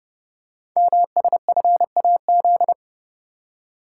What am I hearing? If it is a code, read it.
Morse code, heard as a keyed tone, about 28 wpm, 700 Hz. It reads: MSFAZ
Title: Code Groups - 5 Letters - 30wpm